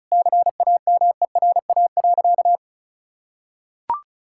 Morse code, heard as a keyed tone, about 34 wpm, 700 Hz. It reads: CAMERA.
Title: Sentences from Top 1000 Words - Review Part 5 - 35wpm